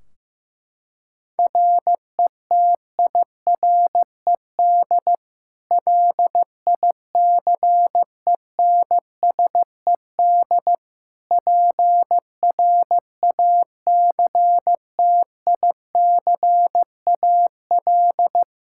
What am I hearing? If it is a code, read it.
Morse code, heard as a keyed tone, about 15 wpm, 700 Hz. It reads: RETIRED LICENSED PRACTICAL
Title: Practice QSO - Exchange 3 - More Info - 15wpm